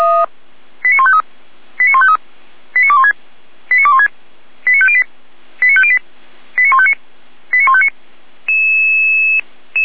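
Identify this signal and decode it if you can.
Five-tone selective call (ZVEI), heard as quick runs of five tones, tones of about 70 ms. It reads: 89153 89153 89216 89216 89508 89508 89159 89159
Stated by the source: ZVEI five-tone selective call 89099, received by SDR.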